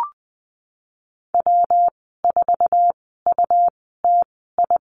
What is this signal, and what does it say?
Morse code, heard as a keyed tone, about 20 wpm, spaced as for 15 wpm, 700 Hz. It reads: W4UTI
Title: 167 Lesson 27g - Number 7 - Call Signs 20wpm